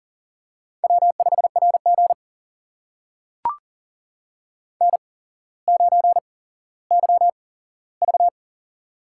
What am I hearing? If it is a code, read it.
Morse code, heard as a keyed tone, about 40 wpm, spaced as for 17 wpm, 700 Hz. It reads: W5RG N9YV
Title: USA Call Signs - 4 Characters - 8x Character Spacing - 40wpm